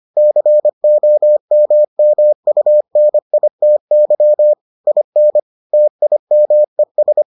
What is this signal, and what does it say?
Morse code, heard as a keyed tone, about 25 wpm, 600 Hz. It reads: COMMUNITY IN TIMES